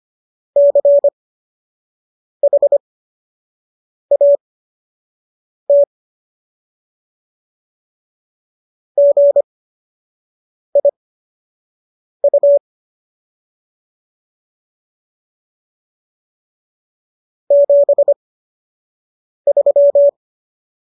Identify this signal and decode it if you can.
Morse code, heard as a keyed tone, about 25 wpm, spaced as for 6 wpm, 570 Hz. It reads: CHAT GIU 73